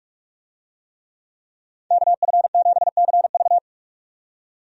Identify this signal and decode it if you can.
Morse code, heard as a keyed tone, about 47 wpm, 700 Hz. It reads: KF7CV
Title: USA Call Signs - 5 Characters - 45wpm